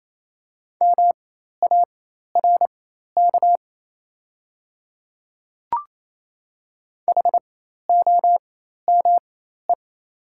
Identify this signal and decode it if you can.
Morse code, heard as a keyed tone, about 28 wpm, spaced as for 13 wpm, 700 Hz. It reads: MARK HOME
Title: 4x Character Spacing - Top 300 Words - 28wpm